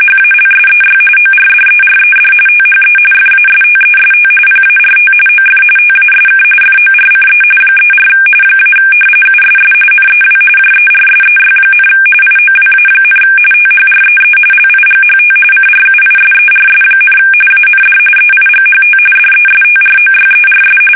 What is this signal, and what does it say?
RTTY, about 45.45 baud. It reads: HALSEY ORDERED THE FASTEST PART OF TASK FORCE 34 - USS NEW JERSEY, IOWA, THREE LIGHT CRUISERS, AND EIGHT DESTROYERS
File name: fsk850 / battleshipnj